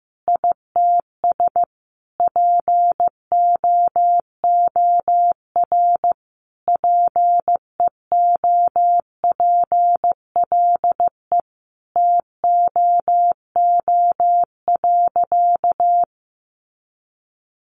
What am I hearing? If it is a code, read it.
Morse code, heard as a keyed tone, about 15 wpm, 700 Hz. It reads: ITS POOR PEOPLE TOO.
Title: Sentences from Top 1000 Words 15wpm